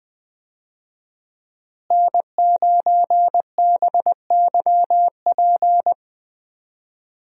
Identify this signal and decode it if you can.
Morse code, heard as a keyed tone, about 20 wpm, 700 Hz. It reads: N9BYP
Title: USA Call Signs - 5 Characters - Repeated 2x - 20wpm